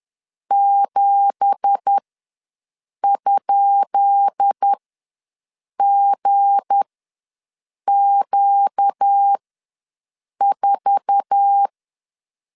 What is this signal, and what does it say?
Morse code, heard as a keyed tone, about 10 wpm, spaced as for 6 wpm, 800 Hz. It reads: 7?GQ4
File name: LEZIONI / 10 SIGNS / 10D COMA ws 10wpm 4314